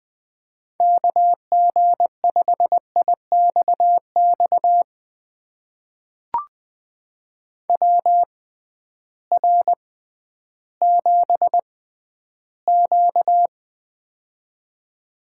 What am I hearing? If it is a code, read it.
Morse code, heard as a keyed tone, about 20 wpm, spaced as for 12 wpm, 700 Hz. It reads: KG5IXX WR7Q
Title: USA Call Signs - 6x Character Spacing - 20wpm